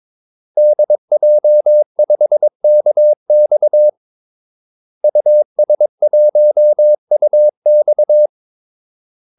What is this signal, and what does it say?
Morse code, heard as a keyed tone, about 22 wpm, 600 Hz. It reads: DJ5KX US1UX
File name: Callsigns-22wpm